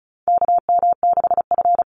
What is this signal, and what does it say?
Morse code, heard as a keyed tone, about 34 wpm, 700 Hz. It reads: KM6F
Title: USA Call Signs 35wpm